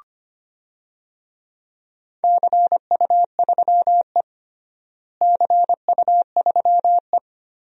Morse code, heard as a keyed tone, about 25 wpm, 700 Hz. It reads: CU3E CU3E
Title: DXCC-100 Call Signs - 4 Characters - Repeated 3x - 25wpm